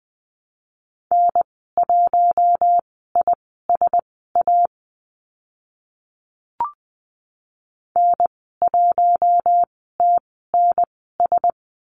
Morse code, heard as a keyed tone, about 20 wpm, spaced as for 14 wpm, 700 Hz. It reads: N1ISA N1TNS
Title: 042 Lesson 8g - Number 1 - Call Signs 20wpm